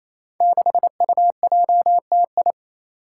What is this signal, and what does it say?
Morse code, heard as a keyed tone, about 28 wpm, 700 Hz. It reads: 6UJTI
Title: Five Challenging Letter-Number Combinations 28wpm